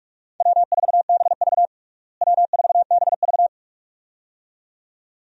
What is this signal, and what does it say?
Morse code, heard as a keyed tone, about 46 wpm, 700 Hz. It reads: W4BV W4BV
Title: USA Call Signs - 4 Characters - Repeated 2x - 45wpm